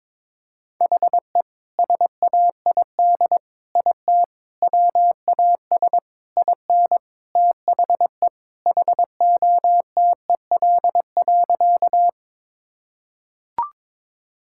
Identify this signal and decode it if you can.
Morse code, heard as a keyed tone, about 22 wpm, 700 Hz. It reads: HE SAID IT WAS IN THE HOTEL.
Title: Sentences from Top 1600 Words 22wpm